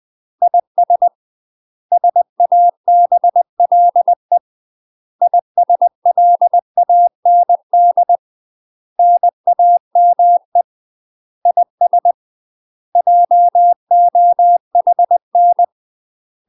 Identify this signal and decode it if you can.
Morse code, heard as a keyed tone, about 20 wpm, 700 Hz. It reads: IS SABLE ISLAND NAME IS JOHN
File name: UK MASTER QSOs TRAINING / 20WPM / UK MASTER QSOs 0043